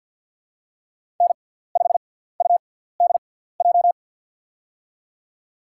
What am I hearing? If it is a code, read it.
Morse code, heard as a keyed tone, about 48 wpm, spaced as for 17 wpm, 690 Hz. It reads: N5UDJ
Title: USA Call Signs - 5 Characters - 6x Character Spacing - 50wpm